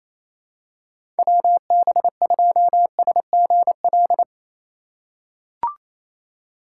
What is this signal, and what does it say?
Morse code, heard as a keyed tone, about 28 wpm, 700 Hz. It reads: WB2SGL
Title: USA Call Signs - 6 Characters - 2x Character Spacing - 28wpm